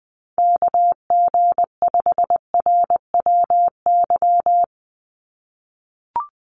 Morse code, heard as a keyed tone, about 20 wpm, 700 Hz. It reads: KG5RWY
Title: USA Call Signs 20wpm